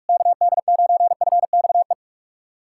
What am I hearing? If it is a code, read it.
Morse code, heard as a keyed tone, about 44 wpm, 690 Hz. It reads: KD9FXE
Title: SOTA QSO Practice 45wpm